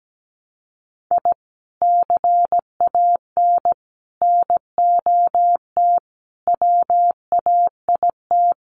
Morse code, heard as a keyed tone, about 17 wpm, 700 Hz. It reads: I CAN NOT WAIT
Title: Sets of 5 Words Encore 17wpm